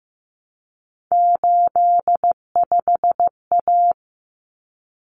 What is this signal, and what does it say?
Morse code, heard as a keyed tone, about 15 wpm, 700 Hz. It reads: 85A
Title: Common Cut Numbers - 3-Digit 15wpm